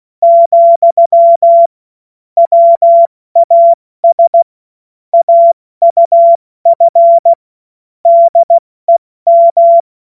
Morse code, heard as a keyed tone, about 16 wpm, 680 Hz. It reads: , WAS AUF DEM
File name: Morsekurs / Witze / Witz 17. August 2021, 70 F80